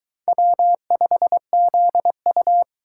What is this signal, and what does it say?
Morse code, heard as a keyed tone, about 23 wpm, 700 Hz. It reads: W5ZU